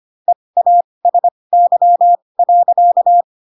Morse code, heard as a keyed tone, about 25 wpm, spaced as for 20 wpm, 700 Hz. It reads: EASY.
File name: Pinocchio / Pinocchio ch8 15 25 ik1hge